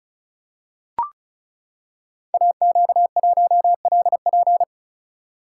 Morse code, heard as a keyed tone, about 36 wpm, 700 Hz. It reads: AQ1LP
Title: Code Groups - 5 Letter:Numbers - 35wpm